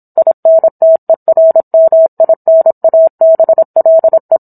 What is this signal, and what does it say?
Morse code, heard as a keyed tone, about 27 wpm, 650 Hz. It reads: INTERMINABLE
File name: Book One / 17-Chapter 17 - 26 WPM